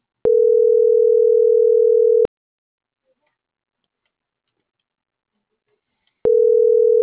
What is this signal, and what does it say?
Ringback tone (North America), 440 + 480 Hz.